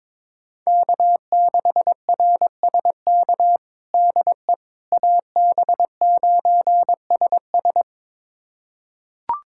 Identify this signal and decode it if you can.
Morse code, heard as a keyed tone, about 22 wpm, 700 Hz. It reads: K6RSK DE AB9SS
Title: USA Callsign DE Callsign - 5 Characters - 22wpm